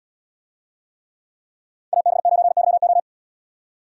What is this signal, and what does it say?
Morse code, heard as a keyed tone, about 77 wpm, 700 Hz. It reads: AB9YO